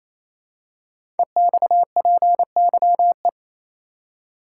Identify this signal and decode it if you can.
Morse code, heard as a keyed tone, about 27 wpm, 700 Hz. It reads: EXPYE
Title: Code Groups - 5 Letters - 28wpm